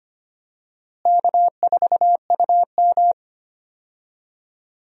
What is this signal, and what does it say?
Morse code, heard as a keyed tone, about 25 wpm, 700 Hz. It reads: K4UM
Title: USA Call Signs - 4 Characters - Repeated 3x - 25wpm